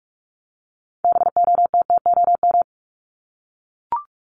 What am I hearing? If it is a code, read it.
Morse code, heard as a keyed tone, about 44 wpm, 700 Hz. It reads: BOTTOM
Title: Top 751-1000 Words 45wpm